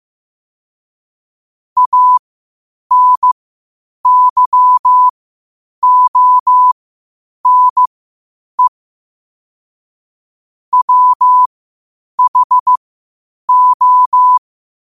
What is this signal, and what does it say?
Morse code, heard as a keyed tone, about 15 wpm, spaced as for 8 wpm, 1000 Hz. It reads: ANYONE WHO
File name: qotdfiles / 8wpm